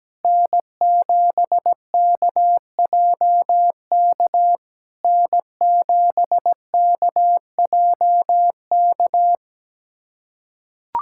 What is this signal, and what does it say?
Morse code, heard as a keyed tone, about 17 wpm, 700 Hz. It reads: N7KJK N7KJK
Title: SOTA QSO Practice 17wpm